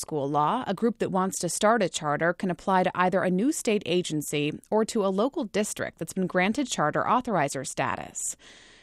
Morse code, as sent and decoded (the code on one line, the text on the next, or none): none